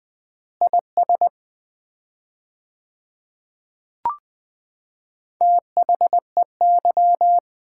Morse code, heard as a keyed tone, about 20 wpm, 700 Hz. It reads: IS THEY
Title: 4x Word-Spacing - Sets of 4 Words - 20wpm